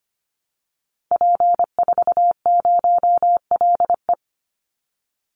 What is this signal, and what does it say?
Morse code, heard as a keyed tone, about 25 wpm, 700 Hz. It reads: P40LE
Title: Call Signs - 25wpm